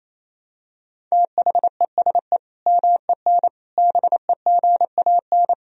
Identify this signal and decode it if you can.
Morse code, heard as a keyed tone, about 28 wpm, 700 Hz. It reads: THESE MEN BEGAN
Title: Sentences from Top 1000 Words - Review Part 1 - 28wpm